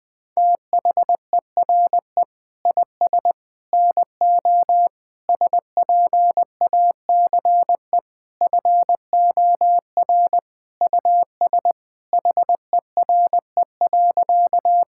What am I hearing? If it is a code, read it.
Morse code, heard as a keyed tone, about 20 wpm, 700 Hz. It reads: THERE IS NO SPACE FOR US HERE.
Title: Sentences from Top 1000 Words - Review Part 5 - 20wpm